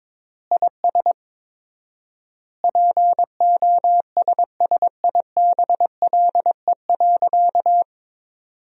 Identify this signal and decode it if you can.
Morse code, heard as a keyed tone, about 21 wpm, 700 Hz. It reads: IS POSSIBLE.
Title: 4x Word-Spacing - Sentences from Top 500 Words - 22wpm